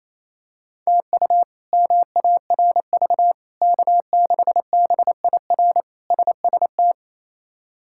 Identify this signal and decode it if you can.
Morse code, heard as a keyed tone, about 28 wpm, 700 Hz. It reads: TU MARV K6BIR SST
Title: K1USN SST QSO Practice - 28wpm